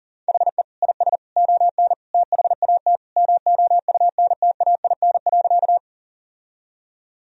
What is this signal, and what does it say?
Morse code, heard as a keyed tone, about 40 wpm, 700 Hz. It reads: HE IS ON THAT MOUNTAIN.
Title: Sentences from Top 1000 Words - Review Part 1 - 40wpm